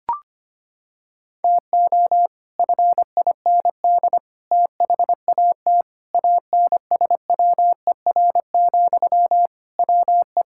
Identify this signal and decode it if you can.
Morse code, heard as a keyed tone, about 25 wpm, 700 Hz. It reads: TO FIND THAT ANSWER, WE
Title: Sentences from Top 1600 Words 25wpm